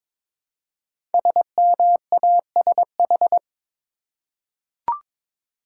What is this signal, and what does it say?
Morse code, heard as a keyed tone, about 22 wpm, 690 Hz. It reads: SMASH
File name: 5 Letter Words - First 3 Letters Given - 22wpm